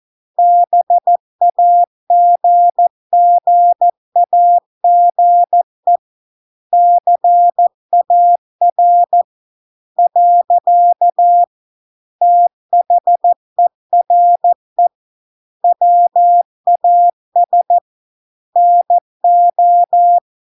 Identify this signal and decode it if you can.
Morse code, heard as a keyed tone, about 14 wpm, 700 Hz. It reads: BAGGAGE CAR . THERE WAS NO